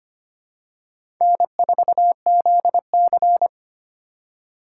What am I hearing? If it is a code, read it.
Morse code, heard as a keyed tone, about 25 wpm, 700 Hz. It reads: N4ZC
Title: USA Call Signs - 4 Characters - 2x Character Spacing - 25wpm